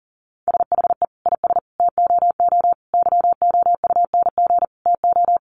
Morse code, heard as a keyed tone, about 40 wpm, 700 Hz. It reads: SHE IS TOO YOUNG TO
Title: Sentences from Top 1300 Words 40wpm